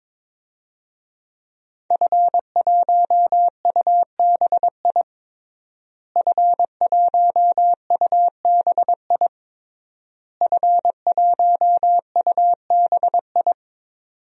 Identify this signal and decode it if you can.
Morse code, heard as a keyed tone, about 22 wpm, 700 Hz. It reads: F1UBI F1UBI F1UBI
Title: DXCC-100 Call Signs - 5 Characters - Repeated 3x - 22wpm